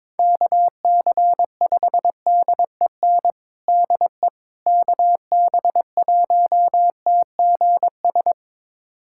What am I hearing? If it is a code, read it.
Morse code, heard as a keyed tone, about 22 wpm, 700 Hz. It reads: KC5DEN DE KB1TGS
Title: USA Callsign DE Callsign - 6 Characters - 22wpm